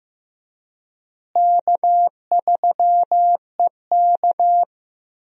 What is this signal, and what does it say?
Morse code, heard as a keyed tone, about 15 wpm, 700 Hz. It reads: K3EK
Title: USA Call Signs - 4 Characters - Repeated 3x - 15wpm